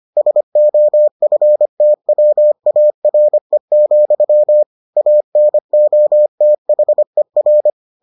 Morse code, heard as a keyed tone, about 25 wpm, 600 Hz. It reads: SOFTWARE, ANOTHER